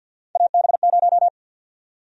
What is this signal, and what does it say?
Morse code, heard as a keyed tone, about 48 wpm, 700 Hz. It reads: AB0
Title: USA Call Sign Prefixes - Repeated 2x - 50wpm